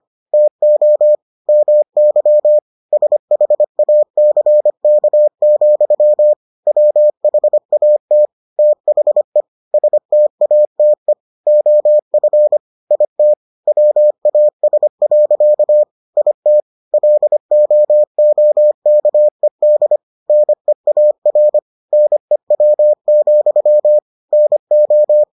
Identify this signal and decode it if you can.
Morse code, heard as a keyed tone, about 25 wpm, 600 Hz. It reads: TO MY SHACK, WHAT THE STATE OF IT WAS. IT LOOKED NEAR NEW, NO